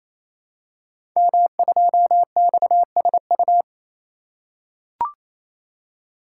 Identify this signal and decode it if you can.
Morse code, heard as a keyed tone, about 29 wpm, 700 Hz. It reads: M2XSU
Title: Code Groups - 5 Letter:Numbers - 28wpm